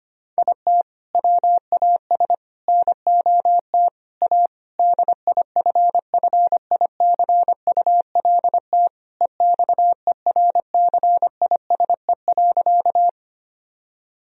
Morse code, heard as a keyed tone, about 25 wpm, 700 Hz. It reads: IT WAS NOT A DIFFICULT EXERCISE.
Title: Sentences from Top 1000 Words - Review Part 4 - 25wpm